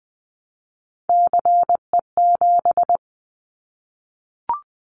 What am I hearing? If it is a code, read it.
Morse code, heard as a keyed tone, about 20 wpm, 700 Hz. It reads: CE7
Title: Callsign Prefixes for Top 100 DXCC - 20wpm